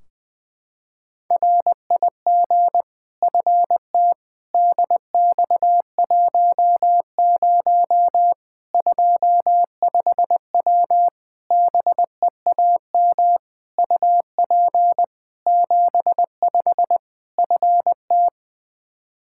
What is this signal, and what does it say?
Morse code, heard as a keyed tone, about 21 wpm, 700 Hz. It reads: RIG FT DX10 25W BEAM UP 75 FT
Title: Practice QSO - Exchange 2 - Info - 20wpm